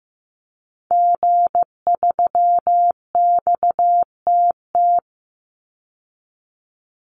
Morse code, heard as a keyed tone, about 15 wpm, 700 Hz. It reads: G3XTT
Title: Call Signs Repeated 2x - 15wpm